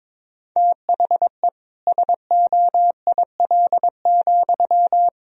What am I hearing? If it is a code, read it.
Morse code, heard as a keyed tone, about 22 wpm, 700 Hz. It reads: THE SOIL,
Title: Sentences from Top 1000 Words - Review Part 4 - 22wpm